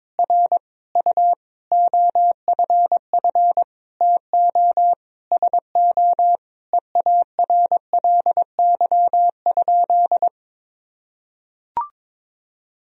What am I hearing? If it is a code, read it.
Morse code, heard as a keyed tone, about 22 wpm, 700 Hz. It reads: R U OFF TO SO EARLY?
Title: Sentences from Top 400 Words and Abbreviations 22wpm